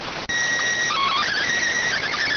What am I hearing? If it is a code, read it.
SSTV (slow-scan television), PD 290.